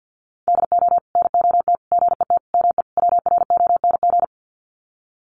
Morse code, heard as a keyed tone, about 50 wpm, 700 Hz. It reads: DO NOT GET ME WRONG
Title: Sets of 5 Words Encore 50wpm